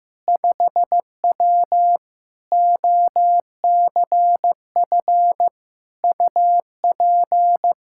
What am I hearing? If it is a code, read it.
Morse code, heard as a keyed tone, about 15 wpm, 700 Hz. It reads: 5W OCF UP
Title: K1USN SST QSO Practice 15wpm